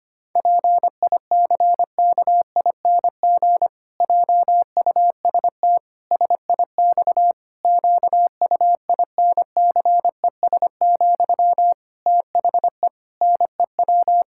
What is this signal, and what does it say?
Morse code, heard as a keyed tone, about 25 wpm, 700 Hz. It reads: PICKING JUST SIX QUINCES, THE NEW
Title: Pangram Challenge v2 - 25wpm